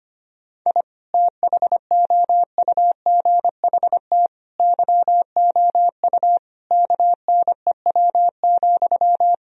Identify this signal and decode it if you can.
Morse code, heard as a keyed tone, about 25 wpm, 700 Hz. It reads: I THOUGHT YOU KNEW,
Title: Sentences from Top 1000 Words - Review Part 4 - 25wpm